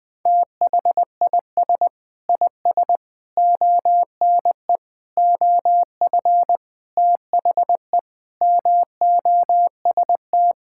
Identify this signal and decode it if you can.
Morse code, heard as a keyed tone, about 21 wpm, 700 Hz. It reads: THIS IS ONE OF THE MOST
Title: Sentences with 5-Word Phrases - Spoken Beginning - 20wpm